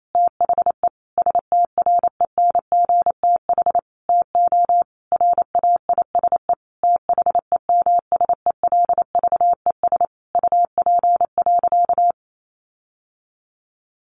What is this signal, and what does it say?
Morse code, heard as a keyed tone, about 28 wpm, 700 Hz. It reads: THE STRENGTH TO RAISE THEMSELVES UP.
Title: Sentences from Top 1400 Words 28wpm